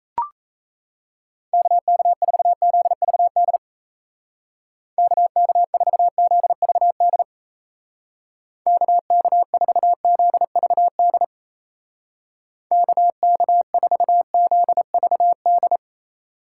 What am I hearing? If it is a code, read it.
Morse code, heard as a keyed tone, about 34 wpm, 700 Hz. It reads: KK4ZVD KK4ZVD KK4ZVD KK4ZVD
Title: Speed Racing - USA 2x3 Call Signs 28wpm